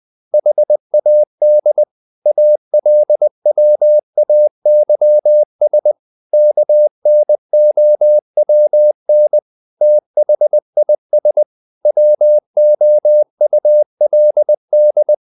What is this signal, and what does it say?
Morse code, heard as a keyed tone, about 20 wpm, 600 Hz. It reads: HAD ALWAYS KNOWN THIS WOULD
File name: the old man and the sea part 2 20wpm